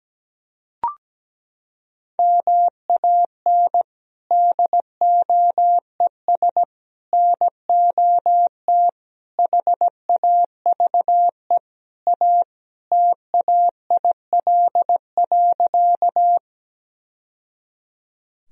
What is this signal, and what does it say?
Morse code, heard as a keyed tone, about 17 wpm, 700 Hz. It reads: MAN DOES NOT HAVE A TAIL.
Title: Sentences from Top 1000 Words - Review Part 3 - 17wpm